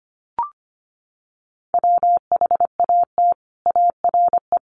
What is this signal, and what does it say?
Morse code, heard as a keyed tone, about 25 wpm, 700 Hz. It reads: WHAT ARE
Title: Sets of 5 Words 25wpm